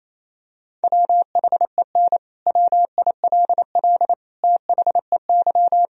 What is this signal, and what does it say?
Morse code, heard as a keyed tone, about 28 wpm, 700 Hz. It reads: WHEN WILL THEY
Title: Sentences from Top 1000 Words - Review Part 4 - 28wpm